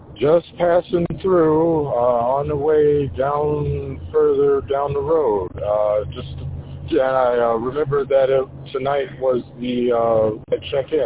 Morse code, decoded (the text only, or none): none